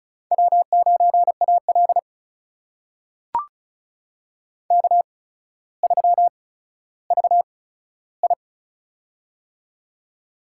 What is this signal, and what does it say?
Morse code, heard as a keyed tone, about 36 wpm, spaced as for 15 wpm, 700 Hz. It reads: W9AL K3VI
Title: USA Call Signs - 4 Characters - 8x Character Spacing - 35wpm